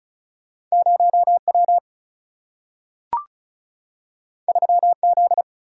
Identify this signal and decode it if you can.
Morse code, heard as a keyed tone, about 36 wpm, 700 Hz. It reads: 0 W 3 Z
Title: Single-Letter Single-Number 35wpm